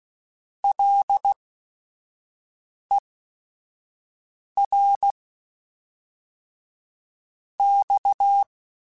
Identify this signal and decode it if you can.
Morse code, heard as a keyed tone, about 16 wpm, 800 Hz. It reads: L E R X
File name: Technical Subjects / Technical Subjects-mp3s / Sw Pwr Sup @5